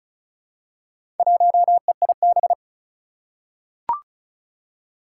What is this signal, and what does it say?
Morse code, heard as a keyed tone, about 36 wpm, 700 Hz. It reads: 1EIB
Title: Four Challenging Letter-Number Combinations 35wpm